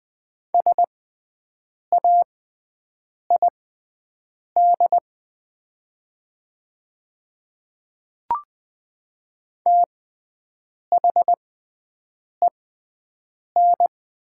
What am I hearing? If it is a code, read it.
Morse code, heard as a keyed tone, about 20 wpm, spaced as for 7 wpm, 700 Hz. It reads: SAID THEN
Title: 6x Character Spacing - Top 100 Words - 20wpm